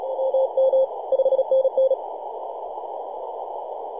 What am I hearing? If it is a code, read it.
Morse code, heard as a keyed tone, about 31 wpm, 530 Hz, with radio noise. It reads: MM 5NN
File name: posts / 2015-oct-03-2319z-tx3x-21-023mhz-cw